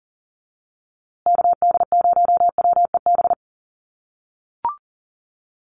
Morse code, heard as a keyed tone, about 37 wpm, 700 Hz. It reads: KD0WEB
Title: USA Call Signs Repeated 2x 40wpm